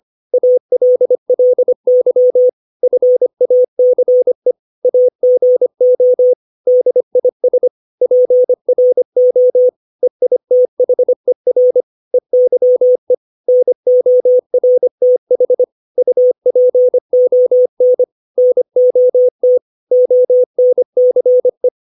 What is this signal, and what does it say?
Morse code, heard as a keyed tone, about 25 wpm, 500 Hz. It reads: ALLY FACE AGO DIS PRO EITHER EYE NORTH UPON NOT ONCE
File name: 632 words / 632 Words 25WPM